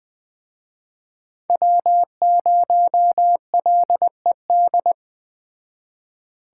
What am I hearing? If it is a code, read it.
Morse code, heard as a keyed tone, about 21 wpm, 700 Hz. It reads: W0LED